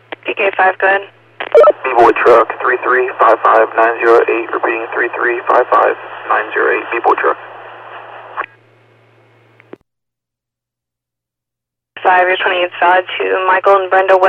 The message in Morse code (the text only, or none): none